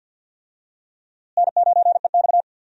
Morse code, heard as a keyed tone, about 52 wpm, 700 Hz. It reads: N9EX
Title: USA Call Signs - 4 Characters - Repeated 3x - 50wpm